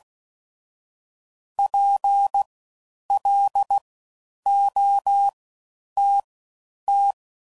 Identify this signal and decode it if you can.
Morse code, heard as a keyed tone, about 16 wpm, spaced as for 9 wpm, 800 Hz. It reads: PLOTT
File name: Suffixes / Special INGs / Fast Ings